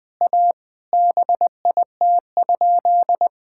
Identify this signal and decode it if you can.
Morse code, heard as a keyed tone, about 20 wpm, 700 Hz. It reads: A BIT?
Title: Sentences with 5-Word Phrases - 20wpm